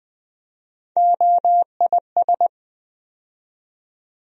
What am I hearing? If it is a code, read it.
Morse code, heard as a keyed tone, about 20 wpm, 700 Hz. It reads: OIS
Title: USA Call Sign Postfixes - 3 Letters 20wpm